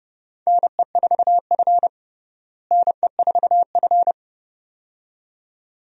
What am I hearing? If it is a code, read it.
Morse code, heard as a keyed tone, about 30 wpm, 700 Hz. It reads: NE4F NE4F
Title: USA Call Signs - 4 Characters - Repeated 2x - 30wpm